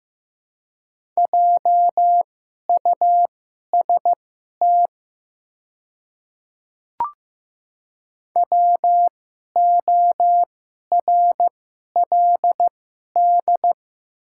Morse code, heard as a keyed tone, about 15 wpm, spaced as for 11 wpm, 700 Hz. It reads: JUST WORLD
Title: 2x Character Spacing - Top 200 Words - 15wpm